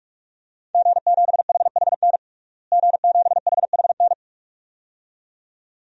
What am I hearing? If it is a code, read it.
Morse code, heard as a keyed tone, about 46 wpm, 700 Hz. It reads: G7HHN G7HHN
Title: DXCC-100 Call Signs - 5 Characters - Repeated 2x - 45wpm